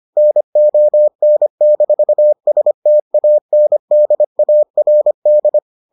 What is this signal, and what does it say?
Morse code, heard as a keyed tone, about 25 wpm, 600 Hz. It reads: NON-STANDARD